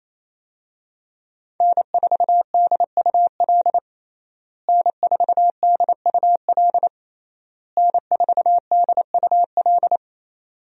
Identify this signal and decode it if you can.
Morse code, heard as a keyed tone, about 28 wpm, 700 Hz. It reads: N4DUL N4DUL N4DUL
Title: USA Call Signs - 5 Characters - Repeated 3x - 28wpm